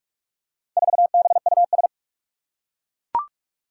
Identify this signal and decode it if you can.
Morse code, heard as a keyed tone, about 41 wpm, 700 Hz. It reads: 4BUS
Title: Four Challenging Letter-Number Combinations 45wpm